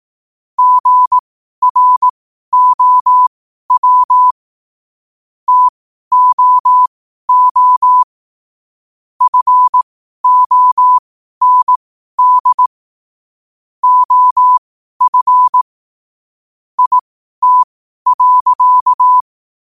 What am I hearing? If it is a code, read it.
Morse code, heard as a keyed tone, about 18 wpm, spaced as for 12 wpm, 1000 Hz. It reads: GROW TOO FOND OF IT.
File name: qotdfiles / 12wpm